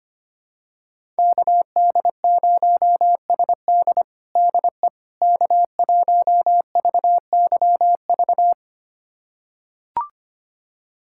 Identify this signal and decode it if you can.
Morse code, heard as a keyed tone, about 25 wpm, 700 Hz. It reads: KD0SD DE K1VYV
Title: USA Callsign DE Callsign 25wpm